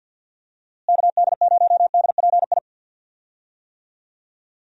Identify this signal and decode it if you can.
Morse code, heard as a keyed tone, about 53 wpm, 700 Hz. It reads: KD0DPI
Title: USA Call Signs - 6 Characters - Repeated 2x - 50wpm